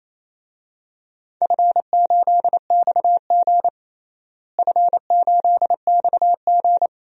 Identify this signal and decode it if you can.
Morse code, heard as a keyed tone, about 27 wpm, 700 Hz. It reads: F8XG F8XG
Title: DXCC-100 Call Signs - 4 Characters - Repeated 2x - 28wpm